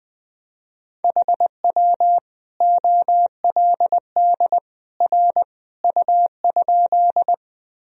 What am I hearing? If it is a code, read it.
Morse code, heard as a keyed tone, about 20 wpm, 700 Hz. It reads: HW OLD R U?
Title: Sentences from Top 200 Words and Abbreviations 20wpm